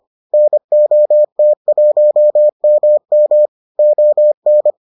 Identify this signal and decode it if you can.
Morse code, heard as a keyed tone, about 25 wpm, 600 Hz. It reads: NOT1MM ON